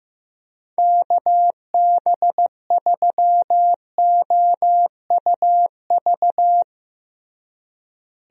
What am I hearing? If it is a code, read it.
Morse code, heard as a keyed tone, about 15 wpm, 700 Hz. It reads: KB3OUV
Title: USA Call Signs - 6 Characters - Repeated 2x - 15wpm